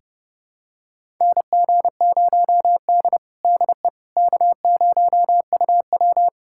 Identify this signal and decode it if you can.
Morse code, heard as a keyed tone, about 30 wpm, 700 Hz. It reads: NG0D DE K0UW
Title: USA Callsign DE Callsign - 4 Characters - 30wpm